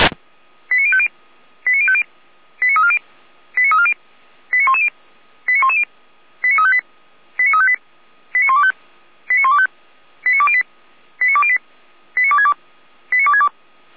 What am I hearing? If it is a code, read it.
Five-tone selective call (ZVEI), heard as quick runs of five tones, tones of about 70 ms. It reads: 89050 89050 89340 89340 89110 89110 89358 89358 89125 89125 89208 89208 89262 89262